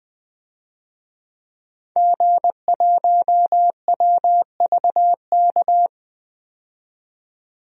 Morse code, heard as a keyed tone, about 20 wpm, 700 Hz. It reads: G1WVK
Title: DXCC-100 Call Signs - 5 Characters - 20wpm